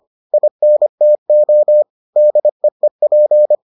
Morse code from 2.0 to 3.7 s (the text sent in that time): DEEP